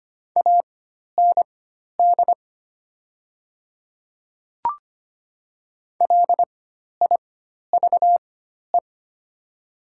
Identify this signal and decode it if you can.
Morse code, heard as a keyed tone, about 25 wpm, spaced as for 12 wpm, 700 Hz. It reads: AND LIVE
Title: 4x Character Spacing - Top 300 Words - 25wpm